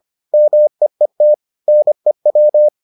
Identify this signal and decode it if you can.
Morse code, heard as a keyed tone, about 25 wpm, 600 Hz. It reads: MEET NEW